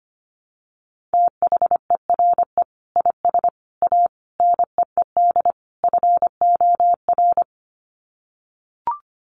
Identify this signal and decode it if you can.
Morse code, heard as a keyed tone, about 25 wpm, 700 Hz. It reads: THERE IS A NEED FOR
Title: Sets of 5 Words Encore2 25wpm